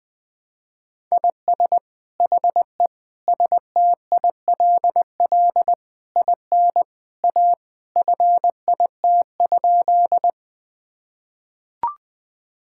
Morse code, heard as a keyed tone, about 20 wpm, 700 Hz. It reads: IS HE STILL IN A FIT?
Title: Sentences from Top 1000 Words - Review Part 5 - 20wpm